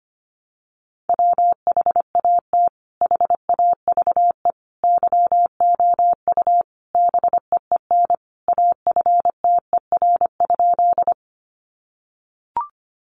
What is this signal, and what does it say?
Morse code, heard as a keyed tone, about 25 wpm, 700 Hz. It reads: WHAT HAVE YOU BEEN AFTER?
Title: Sentences from Top 200 Words 25wpm